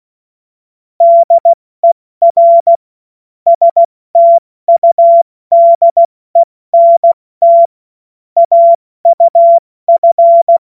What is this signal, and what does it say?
Morse code, heard as a keyed tone, about 16 wpm, 680 Hz. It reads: DER STUDENT AUF
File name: Morsekurs / Witze / Witz 01. August 2023 70 F80